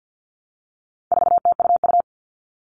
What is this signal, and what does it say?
Morse code, heard as a keyed tone, about 50 wpm, 700 Hz. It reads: 4TUU
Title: All Cut Numbers - 4-Digit 50wpm